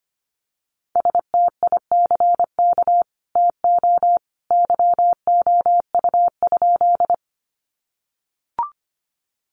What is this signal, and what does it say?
Morse code, heard as a keyed tone, about 25 wpm, 700 Hz. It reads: STICK TO YOU?
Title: Sentences from Top 1200 Words 25wpm